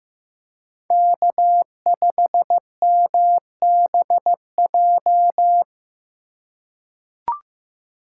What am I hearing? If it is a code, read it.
Morse code, heard as a keyed tone, about 15 wpm, 700 Hz. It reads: K5MBJ
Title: USA Call Signs - 5 Characters - 6x Character Spacing - 15wpm